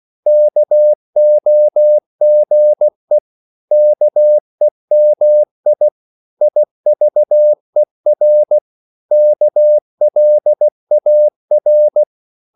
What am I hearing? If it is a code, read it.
Morse code, heard as a keyed tone, about 16 wpm, 600 Hz. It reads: KOGE KEMI IVER KLAR
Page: Korte danske ord 16wpm